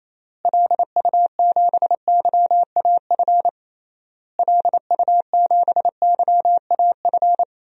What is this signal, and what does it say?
Morse code, heard as a keyed tone, about 28 wpm, 700 Hz. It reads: LU7YAF LU7YAF
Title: DXCC-100 Call Signs - 6 Characters - Repeated 3x - 28wpm